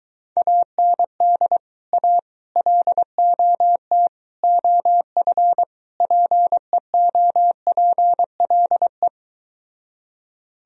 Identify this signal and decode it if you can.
Morse code, heard as a keyed tone, about 23 wpm, 700 Hz. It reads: AND A LOT OF PEOPLE